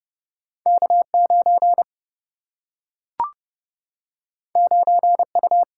Morse code, heard as a keyed tone, about 30 wpm, 700 Hz. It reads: K9 9U
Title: Single-Letter Single-Number 30wpm